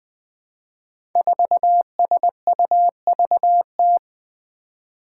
Morse code, heard as a keyed tone, about 20 wpm, 700 Hz. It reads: 4SUVT
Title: Five Challenging Letter-Number Combinations 20wpm